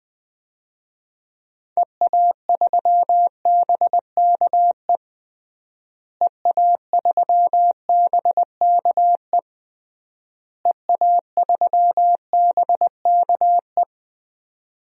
Morse code, heard as a keyed tone, about 20 wpm, 700 Hz. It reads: EA3BKE EA3BKE EA3BKE
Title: DXCC-100 Call Signs - 6 Characters - Repeated 3x - 20wpm